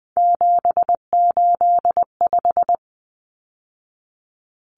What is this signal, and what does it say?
Morse code, heard as a keyed tone, about 20 wpm, 700 Hz. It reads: 785
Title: Numbers - 3-Digit 20wpm